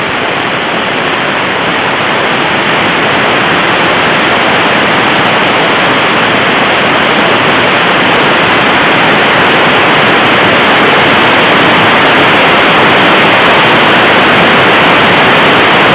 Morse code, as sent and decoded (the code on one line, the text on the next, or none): none